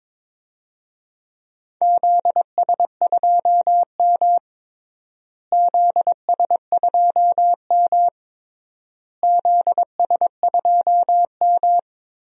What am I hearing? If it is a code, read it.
Morse code, heard as a keyed tone, about 22 wpm, 700 Hz. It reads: ZS2M ZS2M ZS2M
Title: DXCC-100 Call Signs - 4 Characters - Repeated 3x - 22wpm